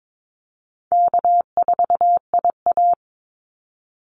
Morse code, heard as a keyed tone, about 22 wpm, 700 Hz. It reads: K4IA
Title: Easy Call Signs Repeated 2x - 22wpm